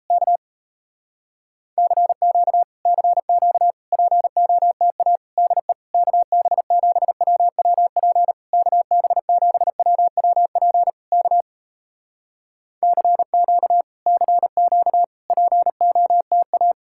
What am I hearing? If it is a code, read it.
Morse code, heard as a keyed tone, about 38 wpm, 700 Hz. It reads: K CQ CQ POTA DE KB7WWP KB7WWP K CQ CQ POTA